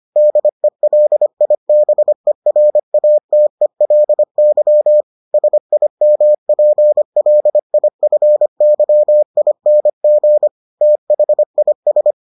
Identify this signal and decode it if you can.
Morse code, heard as a keyed tone, about 25 wpm, 600 Hz. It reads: DELIBERATELY SIMPLIFYING THIS